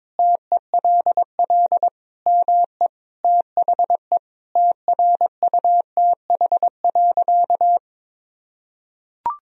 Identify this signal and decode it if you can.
Morse code, heard as a keyed tone, about 22 wpm, 700 Hz. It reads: TELL ME THE TRUTH.
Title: Sentences from Top 1000 Words - Review Part 1 - 22wpm